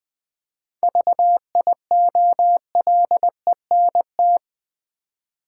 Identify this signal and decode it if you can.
Morse code, heard as a keyed tone, about 21 wpm, 700 Hz. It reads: VIOLENT
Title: 7 Letter Words - First 3 Letters Given - 20wpm